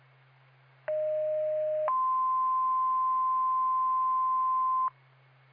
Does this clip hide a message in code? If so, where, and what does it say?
Yes: two-tone page from 0.9 to 4.9 s, heard as a one-second tone, then a three-second tone, about 620 then 1040 Hz.